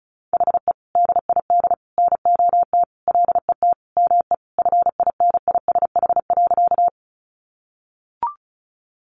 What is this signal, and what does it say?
Morse code, heard as a keyed tone, about 34 wpm, 700 Hz. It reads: HE DID NOT LET ME FINISH.
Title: Sentences from Top 1200 Words 35wpm